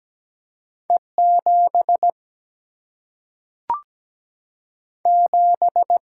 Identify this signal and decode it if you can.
Morse code, heard as a keyed tone, about 17 wpm, 700 Hz. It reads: E 7 7
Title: Single-Letter Single-Number 17wpm